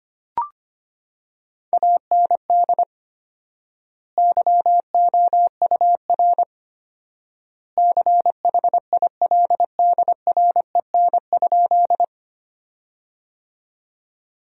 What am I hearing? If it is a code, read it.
Morse code, heard as a keyed tone, about 25 wpm, 700 Hz. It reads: AND YOUR CHILDREN?
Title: 4x Word-Spacing - Sentences from Top 300 Words - 25wpm